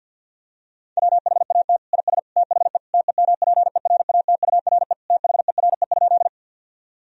Morse code, heard as a keyed tone, about 50 wpm, 700 Hz. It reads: WHAT IS THE TEMPERATURE THERE?
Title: Sentences from Top 1000 Words - Review Part 4 - 50wpm